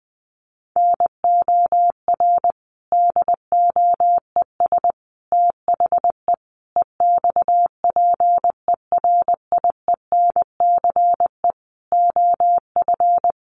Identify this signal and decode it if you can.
Morse code, heard as a keyed tone, about 20 wpm, 700 Hz. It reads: NOR DOES THE EXPERIENCE OF
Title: Sentences from Top 1500 Words 20wpm